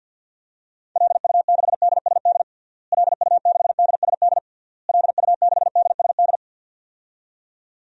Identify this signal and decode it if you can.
Morse code, heard as a keyed tone, about 48 wpm, 700 Hz. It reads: LU6DSD LU6DSD LU6DSD
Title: DXCC-100 Call Signs - 6 Characters - Repeated 3x - 50wpm